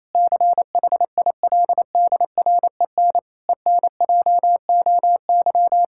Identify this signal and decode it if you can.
Morse code, heard as a keyed tone, about 28 wpm, 700 Hz. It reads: CHILDREN ENJOY
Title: Sentences from Top 1000 Words - Review Part 5 - 28wpm